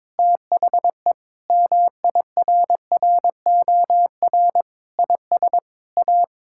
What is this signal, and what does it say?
Morse code, heard as a keyed tone, about 21 wpm, 700 Hz. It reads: THE MIRROR IS A
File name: practice / 22wpm / Sentences with 4-Word Phrases - 22wpm